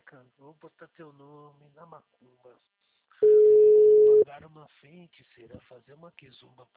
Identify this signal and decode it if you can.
Ringback tone (Europe, 425 Hz), with speech.